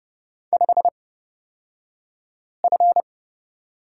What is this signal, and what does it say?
Morse code, heard as a keyed tone, about 30 wpm, spaced as for 5 wpm, 700 Hz. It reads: 5F
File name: Lesson 36d - BT - Two Characters 30wpm